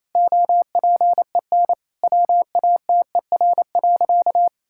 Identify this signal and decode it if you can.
Morse code, heard as a keyed tone, about 28 wpm, 700 Hz. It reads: OPEN WATER.
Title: Sentences from Top 1000 Words - Review Part 6 - 28wpm